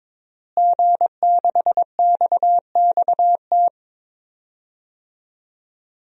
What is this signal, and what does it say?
Morse code, heard as a keyed tone, about 22 wpm, 700 Hz. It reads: G6XXT
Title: DXCC-100 Call Signs - 5 Characters - Repeated 2x - 22wpm